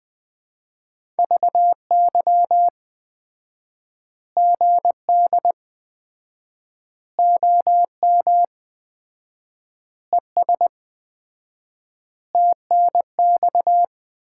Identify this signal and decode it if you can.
Morse code, heard as a keyed tone, about 21 wpm, 700 Hz. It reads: VY GD OM ES TNX
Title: Common QSO Phrases - 4x Word-Spacing - 20wpm